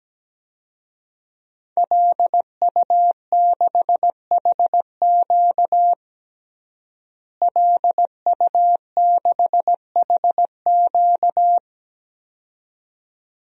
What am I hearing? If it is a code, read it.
Morse code, heard as a keyed tone, about 17 wpm, 700 Hz. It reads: LU6HQ LU6HQ
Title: DXCC-100 Call Signs - 5 Characters - Repeated 2x - 17wpm